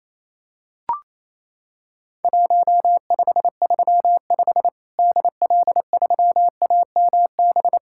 Morse code, heard as a keyed tone, about 28 wpm, 700 Hz. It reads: 1535 DL3AMB
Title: Worked All Europe DX Contest - QTC Exchange - 28wpm